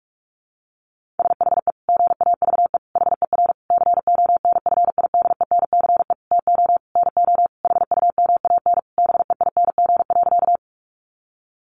Morse code, heard as a keyed tone, about 44 wpm, 700 Hz. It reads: SHE GAVE HER CONFIDENCE TO NO HUMAN BEING.
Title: Sentences from Top 1500 Words 45wpm